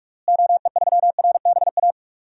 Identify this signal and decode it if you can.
Morse code, heard as a keyed tone, about 46 wpm, 700 Hz. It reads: OE3RBA
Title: DXCC-100 Call Signs - 6 Characters - Repeated 3x - 45wpm